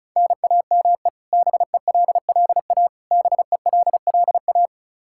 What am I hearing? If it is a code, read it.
Morse code, heard as a keyed tone, about 36 wpm, 700 Hz. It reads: NAME BELLA BELLA
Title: Practice QSO - Exchange 1 - Beginning - 35wpm